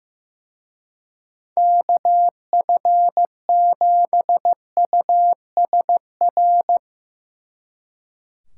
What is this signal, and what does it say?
Morse code, heard as a keyed tone, about 17 wpm, 700 Hz. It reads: KF7USR
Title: Speed Racing - USA 2x3 Call Signs 15wpm